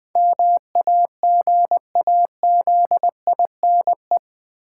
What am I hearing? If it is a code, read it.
Morse code, heard as a keyed tone, about 20 wpm, 700 Hz. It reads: MAGAZINE
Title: Sentences from Top 1000 Words - Review Part 7 - New Word Drill - 20wpm